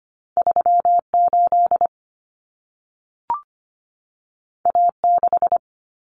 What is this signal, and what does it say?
Morse code, heard as a keyed tone, about 25 wpm, 700 Hz. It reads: 38 A6
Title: Common Cut Numbers - 2-Digit 25wpm